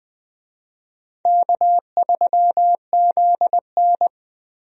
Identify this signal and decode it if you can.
Morse code, heard as a keyed tone, about 20 wpm, 700 Hz. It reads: K3ZN
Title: USA Call Signs - 4 Characters - Repeated 2x - 20wpm